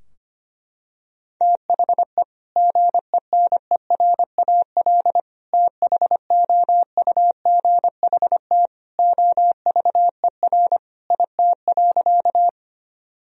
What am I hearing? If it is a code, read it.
Morse code, heard as a keyed tone, about 25 wpm, 700 Hz. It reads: THE GENERAL THOUGHT OVER IT.
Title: Sentences from Top 1000 Words - Review Part 3 - 25wpm